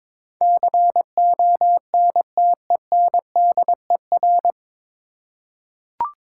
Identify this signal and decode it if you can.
Morse code, heard as a keyed tone, about 22 wpm, 700 Hz. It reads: CONTENDER
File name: 9 Letter Words - First 3 Letters Given - 22wpm